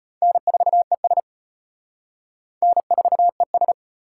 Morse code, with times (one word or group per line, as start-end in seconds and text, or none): 0.2-1.2 N4ES
2.6-3.7 N4ES